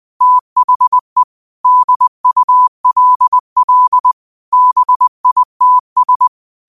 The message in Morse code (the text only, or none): THE DULL BITS